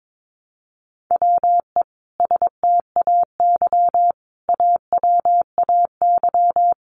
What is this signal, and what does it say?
Morse code, heard as a keyed tone, about 22 wpm, 700 Hz. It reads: WE STAY AWAY
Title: A Sound Of Thunder 22wpm